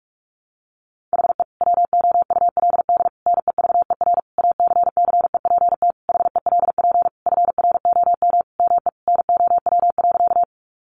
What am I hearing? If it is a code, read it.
Morse code, heard as a keyed tone, about 44 wpm, 700 Hz. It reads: HE WOULD NEVER ACCEPT HELP FROM ME NOW.
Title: Sentences from Top 1300 Words 45wpm